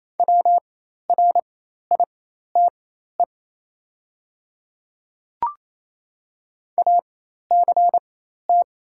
Morse code, heard as a keyed tone, about 28 wpm, spaced as for 13 wpm, 690 Hz. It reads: WRITE ACT
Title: 4x Character Spacing - Top 200 Words - 28wpm